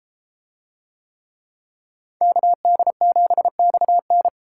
Morse code, heard as a keyed tone, about 32 wpm, 700 Hz. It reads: KD7XN